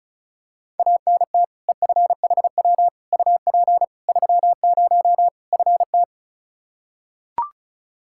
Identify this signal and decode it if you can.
Morse code, heard as a keyed tone, about 36 wpm, 700 Hz. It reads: ANT EFHW UP 30 FT
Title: Practice QSO - Exchange 2 - Info - 35wpm